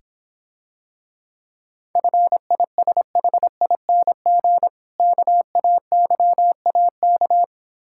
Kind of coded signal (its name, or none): Morse code